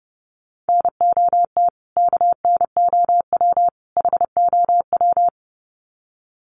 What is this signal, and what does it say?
Morse code, heard as a keyed tone, about 30 wpm, 700 Hz. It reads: NOT KNOW HOW
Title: Sets of 3 Words 30wpm